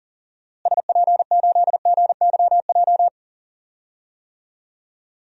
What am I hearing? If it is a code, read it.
Morse code, heard as a keyed tone, about 40 wpm, 700 Hz. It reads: SP8GYJ
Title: DXCC-100 Call Signs - 6 Characters - Repeated 2x - 40wpm